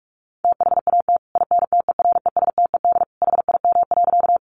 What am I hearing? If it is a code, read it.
Morse code, heard as a keyed tone, about 44 wpm, 700 Hz. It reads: THAT INTERESTED HIM.
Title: Sentences from Top 900 Words 45wpm